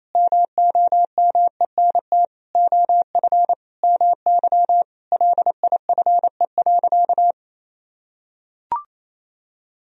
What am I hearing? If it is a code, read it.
Morse code, heard as a keyed tone, about 28 wpm, 700 Hz. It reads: MOMENT OF MY LIFE.
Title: Sentences from Top 1000 Words - Review Part 2 - 28wpm